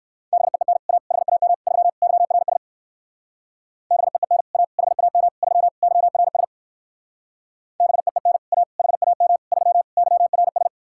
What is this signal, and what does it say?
Morse code, heard as a keyed tone, about 59 wpm, 700 Hz. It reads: BEEN A HAM 3 YRS BEEN A HAM 3 YRS BEEN A HAM 3 YRS